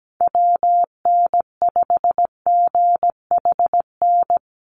Morse code, heard as a keyed tone, about 17 wpm, 700 Hz. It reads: WN5GHN
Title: USA Call Signs Repeated 2x 17wpm